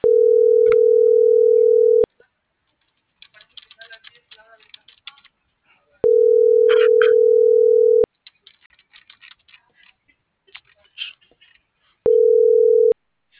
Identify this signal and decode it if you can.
Ringback tone (North America), 440 + 480 Hz.